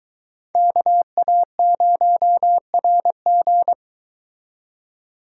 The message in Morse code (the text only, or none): KA0RG